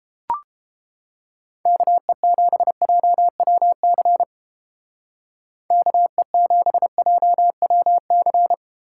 Morse code, heard as a keyed tone, about 31 wpm, 700 Hz. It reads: KE7JWC KE7JWC